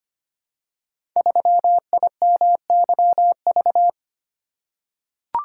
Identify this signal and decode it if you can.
Morse code, heard as a keyed tone, about 25 wpm, 700 Hz. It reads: 3IMYV
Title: Code Groups - 5 Letter:Numbers - 25wpm